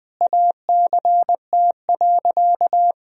Morse code, heard as a keyed tone, about 20 wpm, 700 Hz. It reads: ACT.
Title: Sentences from Top 1000 Words - Review Part 1 - 20wpm